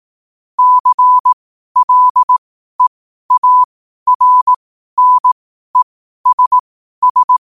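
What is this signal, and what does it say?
Morse code, heard as a keyed tone, about 18 wpm, spaced as for 12 wpm, 1000 Hz. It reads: CLEARNESS
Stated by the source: Morse code quote practice at 12 words per minute.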